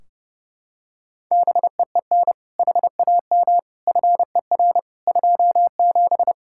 Morse code, heard as a keyed tone, about 30 wpm, 700 Hz. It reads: BEEN HAM FER 27
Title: Practice QSO - Exchange 3 - More Info - 30wpm